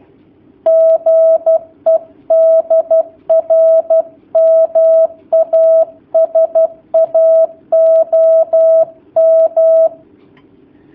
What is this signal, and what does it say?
Morse code, heard as a keyed tone, about 12 wpm, 650 Hz. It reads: GE DR MASA OM
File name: jh8cbh / 2019cw / sp11- w